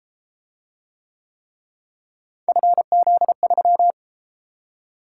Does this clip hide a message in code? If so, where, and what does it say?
Yes: Morse code from 2.5 to 3.9 s, heard as a keyed tone, about 29 wpm, 700 Hz. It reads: FZ3